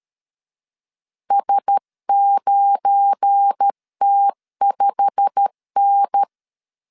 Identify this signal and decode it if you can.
Morse code, heard as a keyed tone, about 13 wpm, 800 Hz. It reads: S9T5N
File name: LEZIONI / 8 YJ45 / 8 YJ45 12wpm 2151